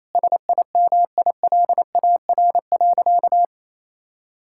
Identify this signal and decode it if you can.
Morse code, heard as a keyed tone, about 28 wpm, 700 Hz. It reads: SIMILAR.
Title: Sentences from Top 1000 Words - Review Part 5 - 28wpm